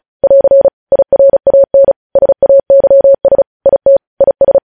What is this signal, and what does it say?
Morse code, heard as a keyed tone, about 36 wpm, 550 Hz. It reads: <AR> IRAN SAYS IT IS
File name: podcast / News headlines-1x-35x35